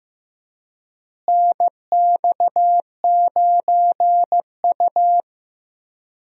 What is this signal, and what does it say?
Morse code, heard as a keyed tone, about 15 wpm, 700 Hz. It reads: NX9U
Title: USA Call Signs - 4 Characters - Repeated 3x - 15wpm